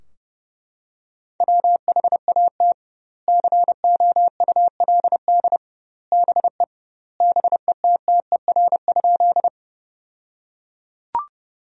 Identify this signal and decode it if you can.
Morse code, heard as a keyed tone, about 30 wpm, 700 Hz. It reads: WHAT COULD BE BETTER?
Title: 2x Word-Spacing - Sentences from Top 400 Words - 30wpm